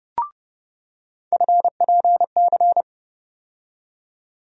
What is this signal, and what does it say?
Morse code, heard as a keyed tone, about 32 wpm, 700 Hz. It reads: FPC
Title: USA Call Sign Postfixes - 3 Letters 30wpm